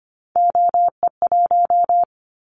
Morse code, heard as a keyed tone, about 25 wpm, 700 Hz. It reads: OE1
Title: Callsign Prefixes for Top 50 DXCC - 25wpm